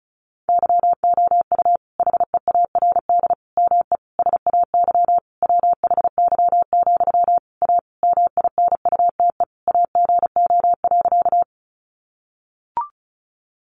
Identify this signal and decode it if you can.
Morse code, heard as a keyed tone, about 34 wpm, 700 Hz. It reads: YOU HEARD ME SAY WHY, A MINUTE AGO.
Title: Sentences from Top 500 Words 35wpm